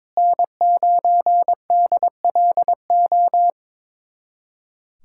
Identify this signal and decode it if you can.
Morse code, heard as a keyed tone, about 22 wpm, 700 Hz. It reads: N9DLO
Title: USA Callsign DE Callsign - 5 Characters - 22wpm